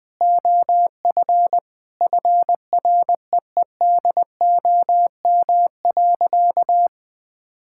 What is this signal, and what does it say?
Morse code, heard as a keyed tone, about 20 wpm, 700 Hz. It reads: OF FREEDOM.
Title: Sentences from Top 1000 Words - Review Part 5 - 20wpm